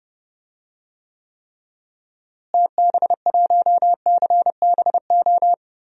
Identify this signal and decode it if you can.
Morse code, heard as a keyed tone, about 30 wpm, 700 Hz. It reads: TB1CBO
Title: DXCC-100 Call Signs - 6 Characters - Repeated 3x - 30wpm